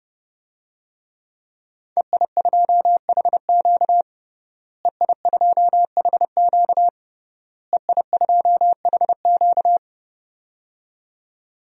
Morse code, heard as a keyed tone, about 30 wpm, 700 Hz. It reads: EI2HQ EI2HQ EI2HQ
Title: DXCC-100 Call Signs - 5 Characters - Repeated 3x - 30wpm